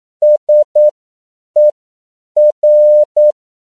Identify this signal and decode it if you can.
Morse code, heard as a keyed tone, about 9 wpm, spaced as for 7 wpm, 600 Hz. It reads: SER
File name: CW / 10PPM